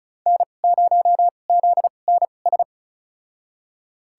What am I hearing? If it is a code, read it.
Morse code, heard as a keyed tone, about 36 wpm, spaced as for 25 wpm, 700 Hz. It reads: N0ZNS
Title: USA Call Signs - 5 Characters - 2x Character Spacing - 35wpm